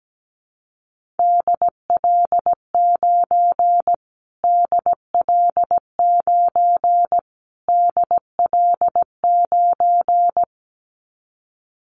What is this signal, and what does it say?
Morse code, heard as a keyed tone, about 17 wpm, 700 Hz. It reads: DL9 DL9 DL9
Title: Callsign Prefixes for Top 25 DXCC - Repeated 3x - 17wpm